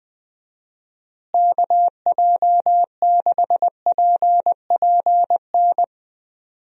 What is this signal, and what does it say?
Morse code, heard as a keyed tone, about 20 wpm, 700 Hz. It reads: KJ6PPN
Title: USA Call Signs - 6 Characters - Repeated 2x - 20wpm